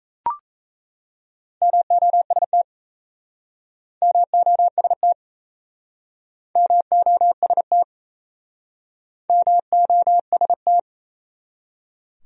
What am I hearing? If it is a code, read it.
Morse code, heard as a keyed tone, about 38 wpm, 700 Hz. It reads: MOST MOST MOST MOST